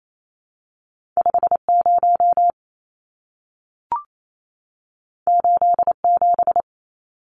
Morse code, heard as a keyed tone, about 28 wpm, 700 Hz. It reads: 50 87
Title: Numbers - 2-Digit 28wpm